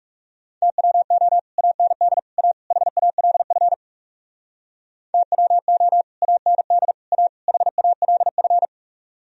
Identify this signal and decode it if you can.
Morse code, heard as a keyed tone, about 36 wpm, 700 Hz. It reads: TWO AND A HALF TWO AND A HALF
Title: Speed-Racing - Sets of 4 Words 30wpm